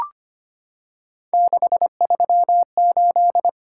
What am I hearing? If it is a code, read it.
Morse code, heard as a keyed tone, about 25 wpm, 700 Hz. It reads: 638
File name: Three Letter-Number Combinations - 25wpm